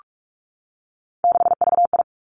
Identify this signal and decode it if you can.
Morse code, heard as a keyed tone, about 46 wpm, 700 Hz. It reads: 6VI
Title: Three Challenging Letter-Number Combinations 45wpm